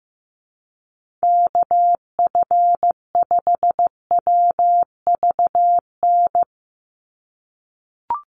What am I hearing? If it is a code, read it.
Morse code, heard as a keyed tone, about 15 wpm, 700 Hz. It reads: KF5WVN
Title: USA Call Signs 15wpm